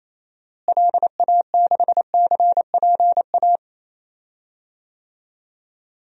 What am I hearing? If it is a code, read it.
Morse code, heard as a keyed tone, about 28 wpm, 700 Hz. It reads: LA6CPA
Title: DXCC-100 Call Signs - 6 Characters - Repeated 2x - 28wpm